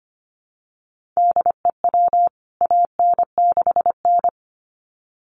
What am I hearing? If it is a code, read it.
Morse code, heard as a keyed tone, about 25 wpm, 700 Hz. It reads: DEW AN6N
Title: CWT Exchange 25wpm